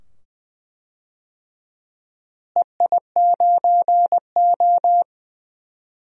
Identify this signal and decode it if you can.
Morse code, heard as a keyed tone, about 20 wpm, 700 Hz. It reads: EI9O
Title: DXCC-100 Call Signs - 4 Characters - Repeated 2x - 20wpm